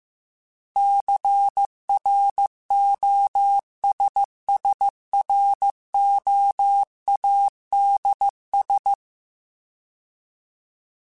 Morse code, heard as a keyed tone, about 15 wpm, 800 Hz. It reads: CROSSROADS